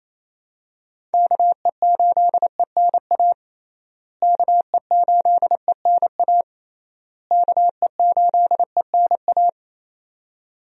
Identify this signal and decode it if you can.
Morse code, heard as a keyed tone, about 28 wpm, 700 Hz. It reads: KE8ENA KE8ENA KE8ENA
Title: USA Call Signs - 6 Characters - Repeated 3x - 28wpm